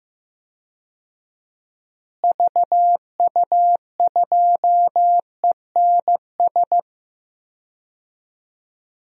Morse code, heard as a keyed tone, about 15 wpm, 700 Hz. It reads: VU2ENS
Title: DXCC-100 Call Signs - 6 Characters - Repeated 3x - 15wpm